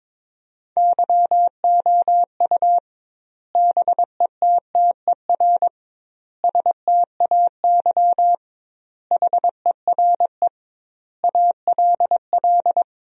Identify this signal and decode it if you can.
Morse code, heard as a keyed tone, about 22 wpm, 700 Hz. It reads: YOU BETTER STAY HERE ALL
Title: 2x Word-Spacing - Sentences from Top 500 Words - 22wpm